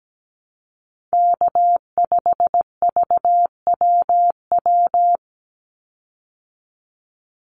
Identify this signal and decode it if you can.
Morse code, heard as a keyed tone, about 17 wpm, 700 Hz. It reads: K5VWW
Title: Call Signs Repeated 2x - 17wpm